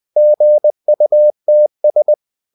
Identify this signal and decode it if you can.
Morse code, heard as a keyed tone, about 20 wpm, 600 Hz. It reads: GUTS
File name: the old man and the sea part 2 20wpm